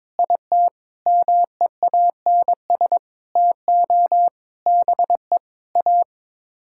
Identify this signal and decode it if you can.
Morse code, heard as a keyed tone, about 22 wpm, 700 Hz. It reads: IT MEANS TO BE A
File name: practice / 22wpm / Sets of 6 Words - 22wpm